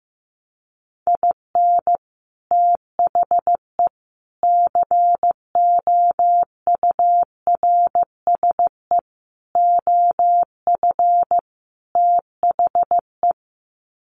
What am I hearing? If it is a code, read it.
Morse code, heard as a keyed tone, about 15 wpm, 700 Hz. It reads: IN THE COURSE OF THE
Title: Sets of 5 Words 15wpm